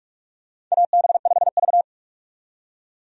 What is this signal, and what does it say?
Morse code, heard as a keyed tone, about 46 wpm, 700 Hz. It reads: AB5V
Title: USA Call Signs - 4 Characters - 45wpm